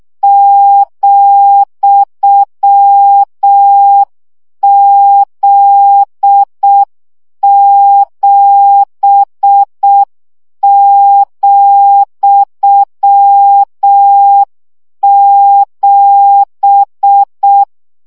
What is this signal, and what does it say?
Morse code, heard as a keyed tone, about 6 wpm, 800 Hz. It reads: ,Z7,7
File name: CW gyakorlas z 7 vesszo 5 baud